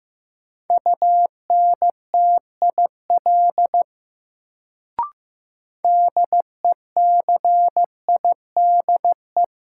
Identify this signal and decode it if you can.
Morse code, heard as a keyed tone, about 15 wpm, 700 Hz. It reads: UNTIL DECIDE
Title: Sentences from Top 500 Words - New Word Drill - 15wpm